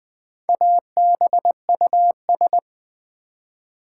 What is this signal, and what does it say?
Morse code, heard as a keyed tone, about 20 wpm, 700 Hz. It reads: ABUS
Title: Build A Word - 1 Letter At A Time - 5-Letter Words - 20wpm